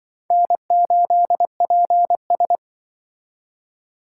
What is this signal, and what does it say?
Morse code, heard as a keyed tone, about 25 wpm, 700 Hz. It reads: N8PS
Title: Speed Racing - USA 1x2 Call Signs 20wpm